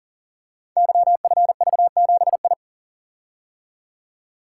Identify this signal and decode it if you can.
Morse code, heard as a keyed tone, about 42 wpm, 690 Hz. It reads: YFV7I
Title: Code Groups - 5 Letter:Numbers - 40wpm